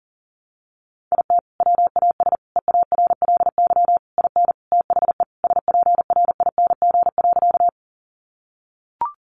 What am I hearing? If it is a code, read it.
Morse code, heard as a keyed tone, about 40 wpm, 700 Hz. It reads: IT WAS EARLY IN THE SPRING.
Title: Sentences from Top 1200 Words 40wpm